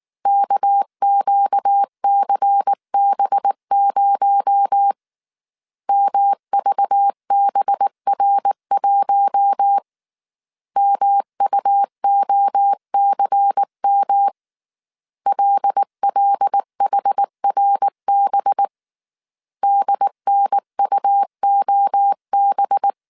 Morse code, heard as a keyed tone, about 19 wpm, 800 Hz. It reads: KQCB0 MVBR1 MUOCM LLHRB DNUOB